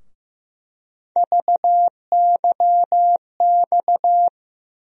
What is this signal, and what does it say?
Morse code, heard as a keyed tone, about 15 wpm, 700 Hz. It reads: VYX
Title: USA Call Sign Postfixes - 3 Letters 15wpm